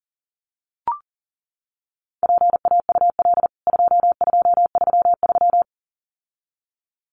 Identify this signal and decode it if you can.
Morse code, heard as a keyed tone, about 40 wpm, 700 Hz. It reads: PAUL 2233
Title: CWT Exchange 40wpm